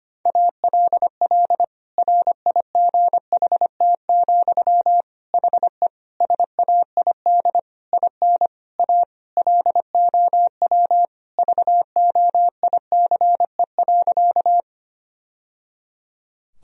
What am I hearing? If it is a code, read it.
Morse code, heard as a keyed tone, about 25 wpm, 700 Hz. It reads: ALL RIGHT, HE SAID IN A LOW VOICE.
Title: Sentences from Top 1000 Words - Review Part 1 - 25wpm